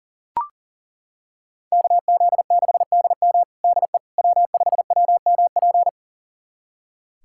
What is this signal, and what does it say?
Morse code, heard as a keyed tone, about 40 wpm, 700 Hz. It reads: KZ6DM DE W5WMP
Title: USA Callsign DE Callsign - 5 Characters - 40wpm